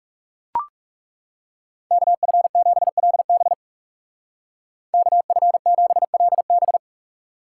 Morse code, heard as a keyed tone, about 37 wpm, 700 Hz. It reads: KF7LB KF7LB
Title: Speed Racing - USA 2x2 Call Signs 30wpm